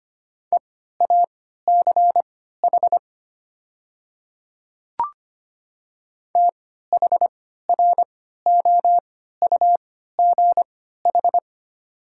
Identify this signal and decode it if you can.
Morse code, heard as a keyed tone, about 25 wpm, spaced as for 14 wpm, 700 Hz. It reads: EACH THROUGH
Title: 3x Character Spacing - Top 200 Words - 25wpm